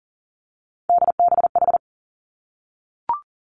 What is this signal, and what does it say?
Morse code, heard as a keyed tone, about 40 wpm, 700 Hz. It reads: DBH
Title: Three Challenging Letter-Number Combinations 40wpm